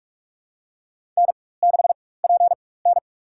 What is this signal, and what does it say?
Morse code, heard as a keyed tone, about 46 wpm, spaced as for 21 wpm, 700 Hz. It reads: N6PN
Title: USA Call Signs - 4 Characters - 4x Character Spacing - 45wpm